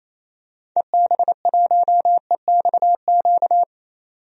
Morse code, heard as a keyed tone, about 28 wpm, 700 Hz. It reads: EB1EXQ
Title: DXCC-100 Call Signs - 6 Characters - 28wpm